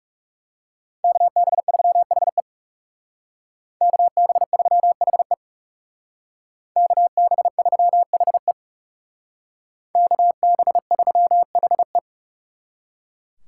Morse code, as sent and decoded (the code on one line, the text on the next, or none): -.- -... ...-- .... / -.- -... ...-- .... / -.- -... ...-- .... / -.- -... ...-- .... .
KB3HE KB3HE KB3HE KB3HE